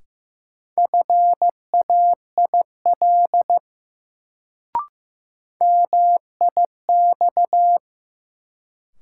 Morse code, heard as a keyed tone, about 15 wpm, 700 Hz. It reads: FAIL MIX
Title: Sentences from Top 1100 Words - New Word Drill Alt - 15wpm